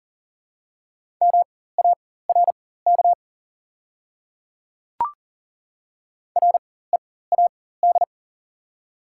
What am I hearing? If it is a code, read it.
Morse code, heard as a keyed tone, about 42 wpm, spaced as for 19 wpm, 690 Hz. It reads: MARK READ
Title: 4x Character Spacing - Top 300 Words - 40wpm